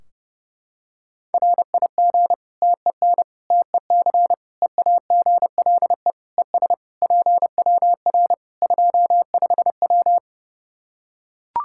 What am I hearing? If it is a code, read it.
Morse code, heard as a keyed tone, about 30 wpm, 700 Hz. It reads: RIG TEN TEC EAGLE ES PWR 25W
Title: K1USN SST QSO Practice 30wpm